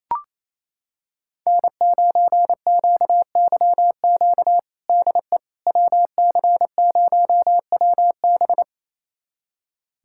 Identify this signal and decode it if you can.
Morse code, heard as a keyed tone, about 28 wpm, 700 Hz. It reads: N9QYQ DE WC0WB
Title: USA Callsign DE Callsign - 5 Characters - 28wpm